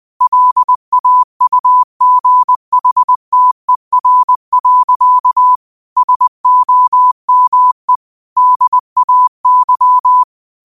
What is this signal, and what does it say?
Morse code, heard as a keyed tone, about 20 wpm, 1000 Hz. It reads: LAUGHTER. SOME DAY
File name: qotdfiles / 20wpm